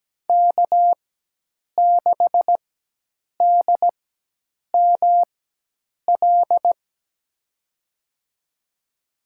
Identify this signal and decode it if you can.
Morse code, heard as a keyed tone, about 17 wpm, spaced as for 8 wpm, 700 Hz. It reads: K6DML
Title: USA Call Signs - 5 Characters - 4x Character Spacing - 17wpm